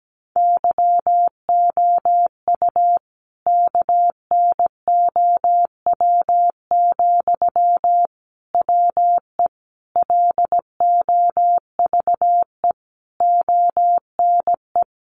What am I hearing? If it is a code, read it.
Morse code, heard as a keyed tone, about 17 wpm, 700 Hz. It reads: YOU KNOW, WE LOVE ONE
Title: Sentences from Top 600 Words 17wpm